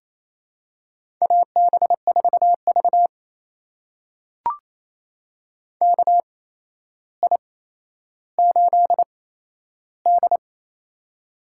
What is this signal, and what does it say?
Morse code, heard as a keyed tone, about 28 wpm, spaced as for 12 wpm, 700 Hz. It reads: AB4V KI8D
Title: USA Call Signs - 4 Characters - 8x Character Spacing - 28wpm